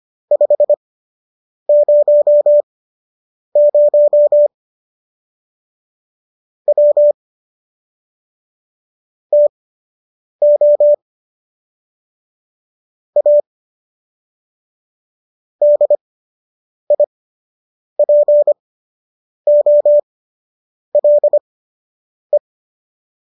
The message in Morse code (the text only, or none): 500 W TO A DIPOLE